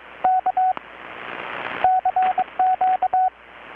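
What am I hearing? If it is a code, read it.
Morse code, heard as a keyed tone, about 23 wpm, 750 Hz. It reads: K CQ
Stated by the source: Morse from KPH's Night of Nights XX maritime radiotelegraphy event, received in Chiapas, Mexico about 3,800 km away on a Kenwood R-600 with a grounded 10 m long-wire antenna.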